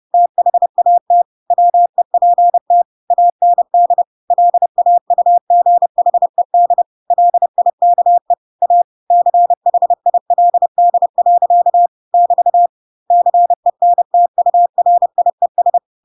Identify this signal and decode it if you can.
Morse code, heard as a keyed tone, about 30 wpm, 700 Hz. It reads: THAT WEPT AND LAUGHED LIKE A CHILD. <BT> CENTURIES
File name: Pinocchio / Pinocchio ch1 30 30 ik1hge